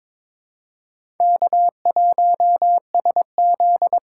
Morse code, heard as a keyed tone, about 21 wpm, 700 Hz. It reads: K1SZ
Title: USA Call Signs - 2x Character Spacing - 22wpm